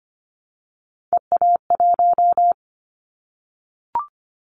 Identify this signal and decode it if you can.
Morse code, heard as a keyed tone, about 25 wpm, 700 Hz. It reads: EA1
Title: Callsign Prefixes for Top 25 DXCC - 25wpm